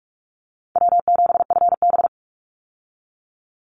Morse code, heard as a keyed tone, about 44 wpm, 700 Hz. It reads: W7FB
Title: Call Signs - 45wpm